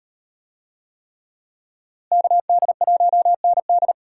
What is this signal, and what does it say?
Morse code, heard as a keyed tone, about 36 wpm, 700 Hz. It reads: KD1ND